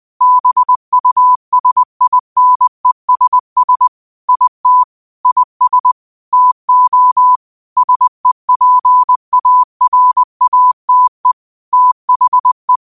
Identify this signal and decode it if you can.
Morse code, heard as a keyed tone, about 20 wpm, 1000 Hz. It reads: BUSINESS IT IS TO SEPARATE THE
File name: qotdfiles / 20wpm